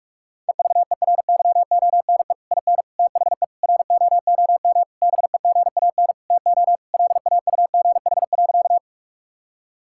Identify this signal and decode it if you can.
Morse code, heard as a keyed tone, about 46 wpm, 700 Hz. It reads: EVERYONE IN THE ROOM BEGAN TO LAUGH.
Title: Sentences from Top 1000 Words - Review Part 5 - 45wpm